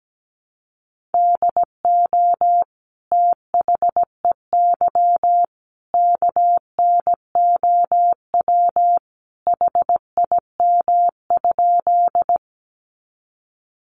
Morse code, heard as a keyed tone, about 17 wpm, 700 Hz. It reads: DO THEY KNOW HIM?
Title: Sentences from Top 100 Words 17wpm